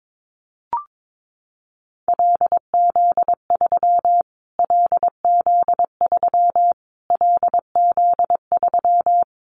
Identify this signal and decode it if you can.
Morse code, heard as a keyed tone, about 22 wpm, 700 Hz. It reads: LZ3 LZ3 LZ3
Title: Callsign Prefixes for Top 25 DXCC - Repeated 3x - 22wpm